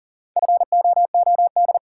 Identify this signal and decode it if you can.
Morse code, heard as a keyed tone, about 37 wpm, 700 Hz. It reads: FOOD
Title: Sentences from Top 300 Words - New Word Drill - 40wpm